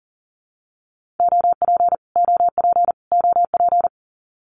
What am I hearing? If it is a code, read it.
Morse code, heard as a keyed tone, about 40 wpm, 700 Hz. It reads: OP OP OP
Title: Abbreviations Repeated 3x - 40wpm